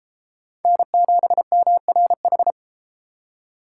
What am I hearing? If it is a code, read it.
Morse code, heard as a keyed tone, about 31 wpm, 700 Hz. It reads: N7MRH